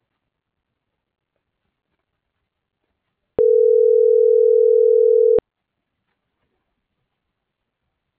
Ringback tone (North America), 440 + 480 Hz.